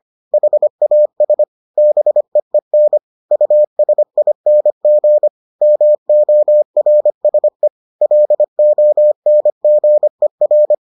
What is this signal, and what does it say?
Morse code, heard as a keyed tone, about 25 wpm, 600 Hz. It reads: HAS BEEN USING MORSE LONGER